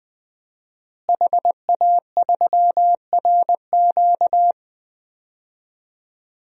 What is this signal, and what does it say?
Morse code, heard as a keyed tone, about 20 wpm, 700 Hz. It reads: HA3RQ
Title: DXCC-100 Call Signs - 5 Characters - Repeated 2x - 20wpm